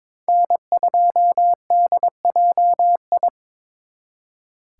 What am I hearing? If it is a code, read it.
Morse code, heard as a keyed tone, about 26 wpm, 700 Hz. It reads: N2DJI
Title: Speed Racing - USA Call Signs 22wpm